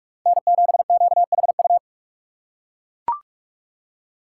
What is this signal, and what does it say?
Morse code, heard as a keyed tone, about 44 wpm, 700 Hz. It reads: N7QHU